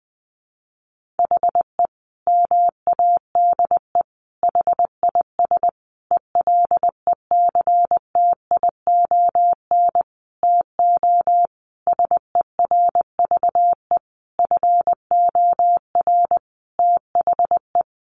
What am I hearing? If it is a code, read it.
Morse code, heard as a keyed tone, about 20 wpm, 700 Hz. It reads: HE MADE HIS ELECTION TO SERVE FOR THE
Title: Sentences from Top 1500 Words 20wpm